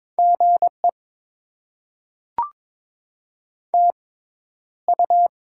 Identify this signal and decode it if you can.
Morse code, heard as a keyed tone, about 22 wpm, spaced as for 11 wpm, 700 Hz. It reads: GE TU
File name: Top 20 QSO Elements - 6x Character Spacing - 22wpm